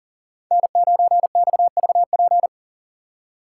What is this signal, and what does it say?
Morse code, heard as a keyed tone, about 50 wpm, 700 Hz. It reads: N9XVP
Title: Speed Racing - USA Call Signs 40wpm